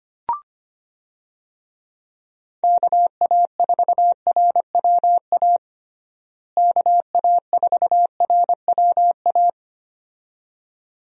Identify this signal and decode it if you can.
Morse code, heard as a keyed tone, about 25 wpm, 700 Hz. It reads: KA4RWA KA4RWA
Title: USA Call Signs - 6 Characters - Repeated 2x - 25wpm